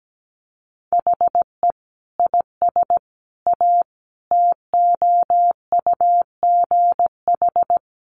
Morse code, heard as a keyed tone, about 17 wpm, 700 Hz. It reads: HE IS A TOUGH
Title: Sentences from Top 1200 Words 17wpm